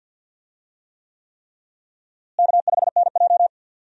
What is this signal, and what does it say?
Morse code, heard as a keyed tone, about 46 wpm, 700 Hz. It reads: K5NJ
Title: USA Call Signs - 4 Characters - Repeated 3x - 50wpm